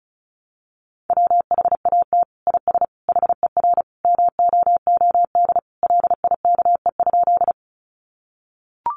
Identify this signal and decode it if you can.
Morse code, heard as a keyed tone, about 34 wpm, 700 Hz. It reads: WHAT IS HER MOOD LIKE?
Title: Sentences from Top 1100 Words 35wpm